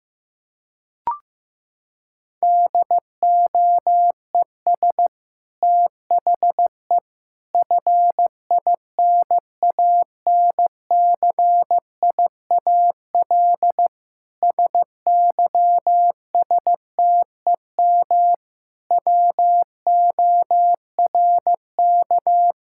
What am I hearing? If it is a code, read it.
Morse code, heard as a keyed tone, about 15 wpm, 700 Hz. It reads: DOES THE FINANCIAL SYSTEM WORK
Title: Sentences from Top 1000 Words - Review Part 5 - 15wpm